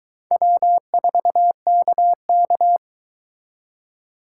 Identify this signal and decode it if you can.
Morse code, heard as a keyed tone, about 23 wpm, 700 Hz. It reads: W4KK